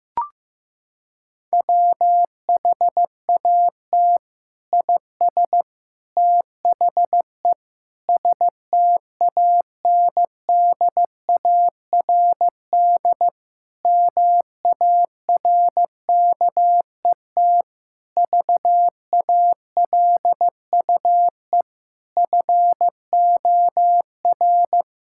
Morse code, heard as a keyed tone, about 16 wpm, 700 Hz. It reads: WHAT IS THE STANDARD MARKET VALUE FOR
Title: Sentences from Top 1000 Words - Review Part 6 - 15wpm